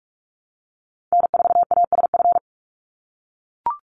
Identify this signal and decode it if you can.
Morse code, heard as a keyed tone, about 44 wpm, 700 Hz. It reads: N4ASF
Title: Call Signs - 45wpm